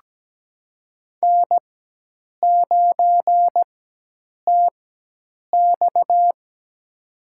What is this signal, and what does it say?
Morse code, heard as a keyed tone, about 17 wpm, spaced as for 8 wpm, 700 Hz. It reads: N9TX
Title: USA Call Signs - 4 Characters - 4x Character Spacing - 17wpm